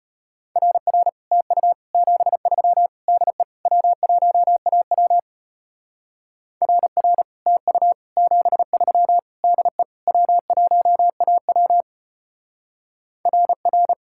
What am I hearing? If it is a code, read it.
Morse code, heard as a keyed tone, about 30 wpm, 700 Hz. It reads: RR TU 73 DE W1AW RR TU 73 DE W1AW RR
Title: Speed-Racing Common QSO Phrases Encore - 25wpm